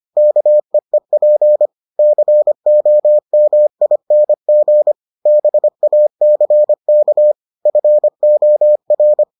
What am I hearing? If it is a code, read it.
Morse code, heard as a keyed tone, about 25 wpm, 600 Hz. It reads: KEEP COMING BACK FOR